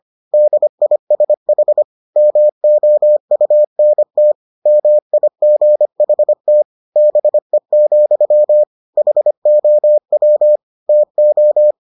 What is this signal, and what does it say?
Morse code, heard as a keyed tone, about 25 wpm, 600 Hz. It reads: DISH MOUNT MIGHT BE, HOW TO